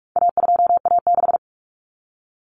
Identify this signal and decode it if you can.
Morse code, heard as a keyed tone, about 44 wpm, 700 Hz. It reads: A2A6
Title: Common Cut Numbers - 4-Digit 45wpm